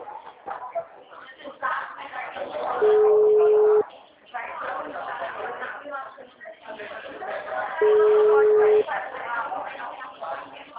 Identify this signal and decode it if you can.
Ringback tone (Europe, 425 Hz), with speech.